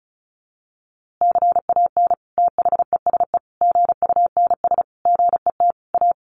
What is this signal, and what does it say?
Morse code, heard as a keyed tone, about 34 wpm, 700 Hz. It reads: CAN THESE GUNS GET A
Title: A Sound Of Thunder 35wpm